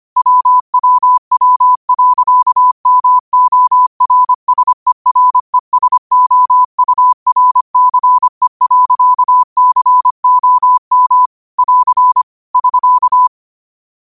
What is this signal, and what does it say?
Morse code, heard as a keyed tone, about 25 wpm, 1000 Hz. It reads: WWW.MORSERESOURCE.COM <AR> <SK>
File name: qotdfiles / 25wpm